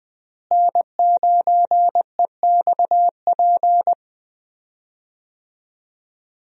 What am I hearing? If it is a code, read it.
Morse code, heard as a keyed tone, about 20 wpm, 700 Hz. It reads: N9EXP
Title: USA Call Signs - 5 Characters - Repeated 3x - 20wpm